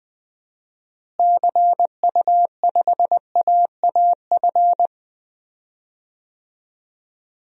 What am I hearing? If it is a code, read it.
Morse code, heard as a keyed tone, about 19 wpm, 700 Hz. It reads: CU5AAF
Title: DXCC-100 Call Signs - 6 Characters - Repeated 3x - 20wpm